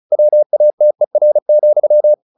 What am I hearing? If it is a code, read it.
Morse code, heard as a keyed tone, about 34 wpm, 600 Hz. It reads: WATER,
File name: the old man and the sea 2 35wpm